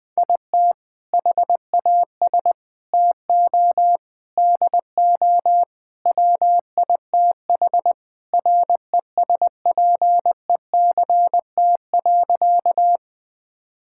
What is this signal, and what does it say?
Morse code, heard as a keyed tone, about 21 wpm, 700 Hz. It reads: IT HAS TO DO WITH RESPECT.
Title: Sentences with 4-Word Phrases - Spoken Beginning - 20wpm